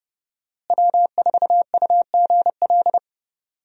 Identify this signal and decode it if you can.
Morse code, heard as a keyed tone, about 30 wpm, 700 Hz. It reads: W4UGL
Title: USA Call Signs - 5 Characters - Repeated 3x - 30wpm